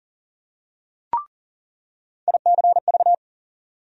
Morse code, heard as a keyed tone, about 40 wpm, 700 Hz. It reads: ICV
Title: USA Call Sign Postfixes - 3 Letters 40wpm